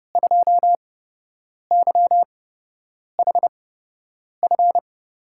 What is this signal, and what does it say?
Morse code, heard as a keyed tone, about 31 wpm, spaced as for 8 wpm, 700 Hz. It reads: 2YHF
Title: USA Call Signs - 5 Characters - 8x Character Spacing - 30wpm